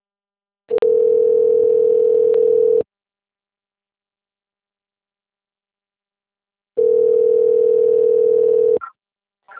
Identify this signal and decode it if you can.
Ringback tone (North America), 440 + 480 Hz.
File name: rg-1600-unknown- - - .34024